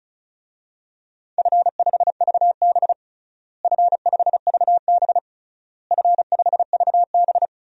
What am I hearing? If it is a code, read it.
Morse code, heard as a keyed tone, about 34 wpm, 700 Hz. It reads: F5VB F5VB F5VB
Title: DXCC-100 Call Signs - 4 Characters - Repeated 3x - 35wpm